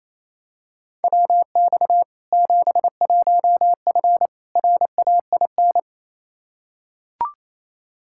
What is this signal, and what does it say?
Morse code, heard as a keyed tone, about 28 wpm, 700 Hz. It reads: WX 71F RAIN
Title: K1USN SST QSO Practice 28wpm